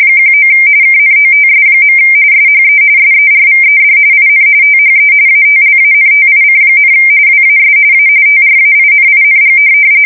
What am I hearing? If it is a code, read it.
RTTY, about 45.45 baud. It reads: 19 KNOTS. SHE SERVED IN THE ATLANTIC, WAS A TRAINING
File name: fsk170 / USS-New-Jersey